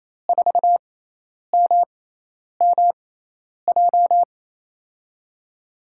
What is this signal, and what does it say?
Morse code, heard as a keyed tone, about 28 wpm, spaced as for 10 wpm, 700 Hz. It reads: 4MMJ
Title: USA Call Signs - 6 Characters - 6x Character Spacing - 28wpm